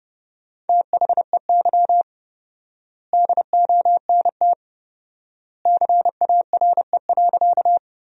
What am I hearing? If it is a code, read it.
Morse code, heard as a keyed tone, about 30 wpm, 700 Hz. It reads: THEY DONT CARE.
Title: 4x Word-Spacing - Sentences from Top 300 Words - 30wpm